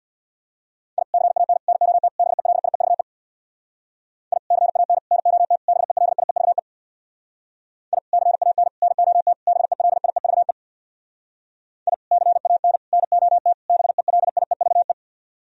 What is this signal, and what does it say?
Morse code, heard as a keyed tone, about 64 wpm, 700 Hz. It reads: I CAN NOT BELIEVE I CAN NOT BELIEVE I CAN NOT BELIEVE I CAN NOT BELIEVE